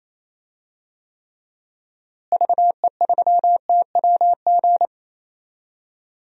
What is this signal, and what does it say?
Morse code, heard as a keyed tone, about 28 wpm, 700 Hz. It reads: VE3TWG
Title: DXCC-100 Call Signs - 6 Characters - Repeated 3x - 28wpm